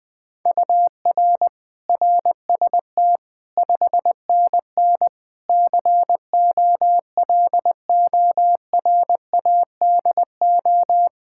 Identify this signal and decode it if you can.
Morse code, heard as a keyed tone, about 21 wpm, 700 Hz. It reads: UR RST 5NN COLORADO
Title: Practice QSO - Exchange 1 - Beginning - 20wpm